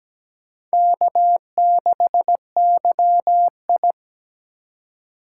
Morse code, heard as a keyed tone, about 17 wpm, 700 Hz. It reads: K6YI
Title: USA Call Signs - 4 Characters - 3x Character Spacing - 17wpm